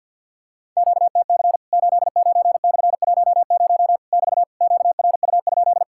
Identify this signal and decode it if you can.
Morse code, heard as a keyed tone, about 50 wpm, 700 Hz. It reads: QTC 89/10 <BT> QRU?
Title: Worked All Europe DX Contest - QTC Exchange with Repeats - 50wpm